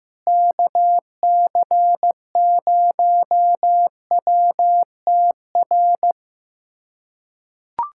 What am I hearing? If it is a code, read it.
Morse code, heard as a keyed tone, about 16 wpm, 700 Hz. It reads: KC0WTR
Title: USA Callsign DE Callsign 15wpm